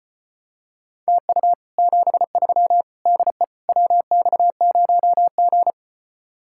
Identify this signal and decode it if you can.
Morse code, heard as a keyed tone, about 29 wpm, 700 Hz. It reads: TU 73 DE WX0G
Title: Speed-Racing Common QSO Phrases Encore - 25wpm